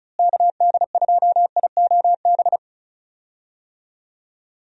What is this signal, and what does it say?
Morse code, heard as a keyed tone, about 34 wpm, 700 Hz. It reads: KD2IOB
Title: USA Call Signs - 6 Characters - Repeated 3x - 35wpm